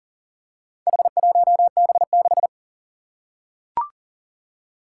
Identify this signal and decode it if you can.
Morse code, heard as a keyed tone, about 39 wpm, 700 Hz. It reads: H1B6
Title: Four Challenging Letter-Number Combinations 40wpm